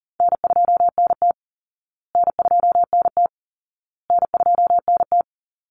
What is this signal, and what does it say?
Morse code, heard as a keyed tone, about 40 wpm, 700 Hz. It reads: N2NT N2NT N2NT
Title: Easy Call Signs Repeated 3x - 40wpm